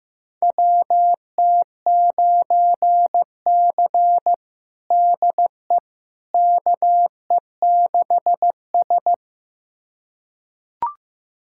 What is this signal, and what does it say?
Morse code, heard as a keyed tone, about 15 wpm, 700 Hz. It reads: WT9C DE KE6S
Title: USA Callsign DE Callsign - 4 Characters - 15wpm